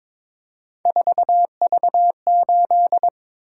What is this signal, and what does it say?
Morse code, heard as a keyed tone, about 22 wpm, 700 Hz. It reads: 4V8
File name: practice / 22wpm / Three Letter-Number Combinations - 22wpm